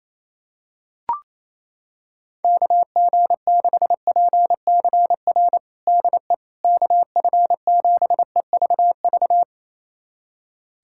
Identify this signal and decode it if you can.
Morse code, heard as a keyed tone, about 27 wpm, 700 Hz. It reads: KG6PCR DE KF7EVV
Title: USA Callsign DE Callsign - 6 Characters - 28wpm